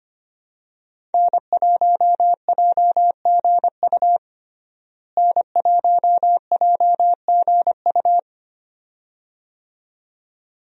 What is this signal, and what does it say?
Morse code, heard as a keyed tone, about 25 wpm, 700 Hz. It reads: N1JGU N1JGU
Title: USA Call Signs - 5 Characters - Repeated 2x - 25wpm